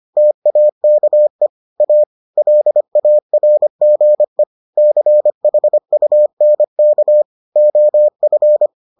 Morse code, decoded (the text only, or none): TAKE A LARGE CHUNK OF